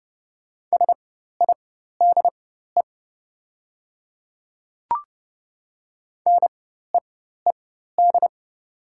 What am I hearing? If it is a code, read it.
Morse code, heard as a keyed tone, about 30 wpm, spaced as for 14 wpm, 700 Hz. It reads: SIDE NEED
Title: 4x Character Spacing - Top 300 Words - 30wpm